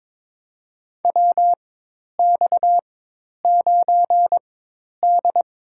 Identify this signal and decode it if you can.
Morse code, heard as a keyed tone, about 23 wpm, spaced as for 10 wpm, 700 Hz. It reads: WX9D
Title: USA Call Signs - 4 Characters - 4x Character Spacing - 22wpm